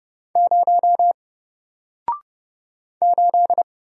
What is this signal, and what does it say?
Morse code, heard as a keyed tone, about 30 wpm, 700 Hz. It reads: 0 8
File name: Single-Letter-Number-30wpm